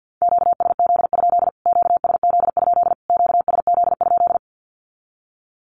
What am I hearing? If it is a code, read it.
Morse code, heard as a keyed tone, about 50 wpm, 700 Hz. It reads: QSZ? QSZ? QSZ?
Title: Q-Codes Repeated 3x 50wpm